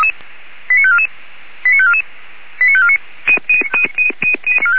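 Five-tone selective call (ZVEI), heard as quick runs of five tones, tones of about 80 ms. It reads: 89544 89544 89549
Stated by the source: Five-tone ZVEI selective call received on an SDR.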